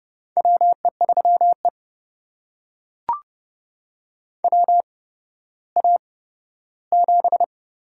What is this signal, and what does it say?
Morse code, heard as a keyed tone, about 30 wpm, spaced as for 15 wpm, 700 Hz. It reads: WE3E WA7
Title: USA Call Signs - 4 Characters - 8x Character Spacing - 30wpm